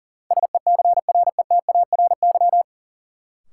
Morse code, heard as a keyed tone, about 40 wpm, 700 Hz. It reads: SECRETARY
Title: Sentences from Top 1300 Words - New Word Drill - 40wpm